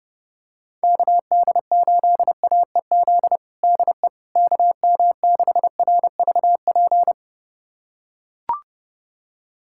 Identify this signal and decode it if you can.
Morse code, heard as a keyed tone, about 30 wpm, 700 Hz. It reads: KD8AEZ DE KM6RVP
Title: USA Callsign DE Callsign - 6 Characters - 30wpm